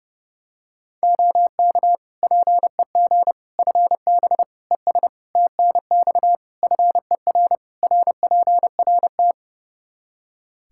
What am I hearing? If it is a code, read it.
Morse code, heard as a keyed tone, about 30 wpm, 700 Hz. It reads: OK PEG FB ES TNX FER RPRT
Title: Practice QSO - Exchange 2 - Info - 30wpm